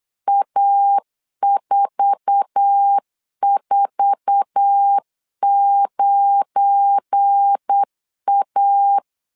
Morse code, heard as a keyed tone, about 8 wpm, 800 Hz. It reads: A449A
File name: LEZIONI / 8 YJ45 / 8 YJ4 8wpm 3246